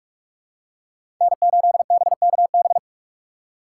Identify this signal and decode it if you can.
Morse code, heard as a keyed tone, about 44 wpm, 700 Hz. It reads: N8BKB
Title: USA Call Signs - 5 Characters - 6x Character Spacing - 45wpm